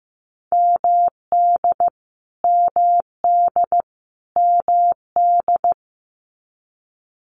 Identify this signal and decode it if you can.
Morse code, heard as a keyed tone, about 15 wpm, 700 Hz. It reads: MD MD MD
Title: US States Repeated 3x 15wpm